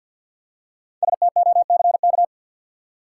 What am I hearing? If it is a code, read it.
Morse code, heard as a keyed tone, about 45 wpm, 700 Hz. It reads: STOCK
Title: Five Letter Words - 50wpm